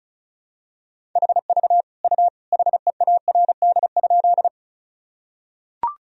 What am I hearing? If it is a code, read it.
Morse code, heard as a keyed tone, about 36 wpm, 700 Hz. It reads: HV U HEARD?
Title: Sentences from Top 400 Words and Abbreviations 35wpm